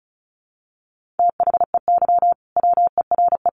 Morse code, heard as a keyed tone, about 34 wpm, 700 Hz. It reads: THEY WERE
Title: Sentences from Top 1400 Words 35wpm